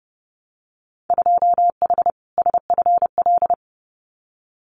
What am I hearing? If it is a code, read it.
Morse code, heard as a keyed tone, about 30 wpm, 700 Hz. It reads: 2H SFL
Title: Winter Field Day Exchanges Encore 30wpm